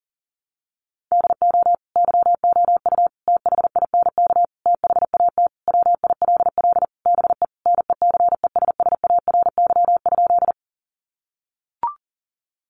Morse code, heard as a keyed tone, about 40 wpm, 700 Hz. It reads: DO YOU THINK THAT WILL BE NECESSARY?
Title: Sentences from Top 800 Words 40wpm